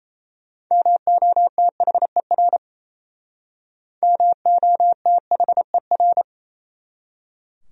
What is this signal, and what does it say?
Morse code, heard as a keyed tone, about 35 wpm, 700 Hz. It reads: MOTHER MOTHER
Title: Speed-Racing Top 500 Words 28wpm